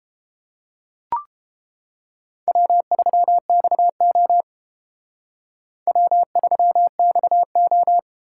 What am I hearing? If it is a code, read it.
Morse code, heard as a keyed tone, about 30 wpm, 700 Hz. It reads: W3XO W3XO